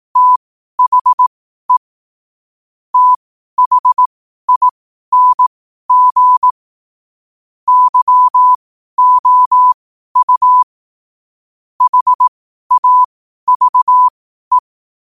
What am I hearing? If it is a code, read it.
Morse code, heard as a keyed tone, about 18 wpm, spaced as for 12 wpm, 1000 Hz. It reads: THE THING YOU HAVE
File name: qotdfiles / 12wpm